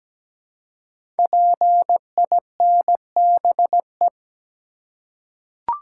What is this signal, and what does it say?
Morse code, heard as a keyed tone, about 17 wpm, 700 Hz. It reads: PINBE
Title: Code Groups - 5 Letter:Numbers - 17wpm